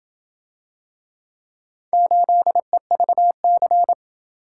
Morse code, heard as a keyed tone, about 28 wpm, 700 Hz. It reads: 8EVC